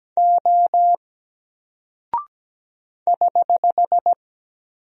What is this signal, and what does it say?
Morse code, heard as a keyed tone, about 17 wpm, 700 Hz. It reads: O <HH>
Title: Single Character (Letter:Number:Punctuation:Prosign) 17wpm